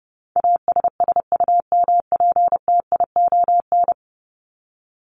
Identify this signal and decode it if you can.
Morse code, heard as a keyed tone, about 30 wpm, 700 Hz. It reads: ASSUMPTION
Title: Top 1751-2000 Words 30wpm